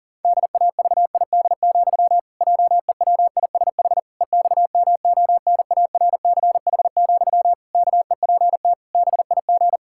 Morse code, heard as a keyed tone, about 40 wpm, 700 Hz. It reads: DAVID, JEWISH EXMONARCH, KEPT BIG
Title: Pangram Challenge v2 - 40wpm